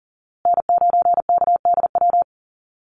Morse code, heard as a keyed tone, about 40 wpm, 700 Hz. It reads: N9KDW
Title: USA Call Signs Repeated 3x 40wpm